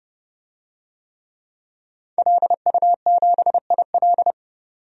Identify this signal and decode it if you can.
Morse code, heard as a keyed tone, about 30 wpm, 700 Hz. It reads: LU7IL
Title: DXCC-100 Call Signs - 5 Characters - 30wpm